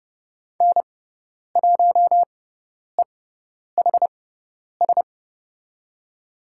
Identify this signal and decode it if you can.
Morse code, heard as a keyed tone, about 30 wpm, spaced as for 10 wpm, 700 Hz. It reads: N1EHS
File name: Lesson 18g - Letter m - Call Signs 30wpm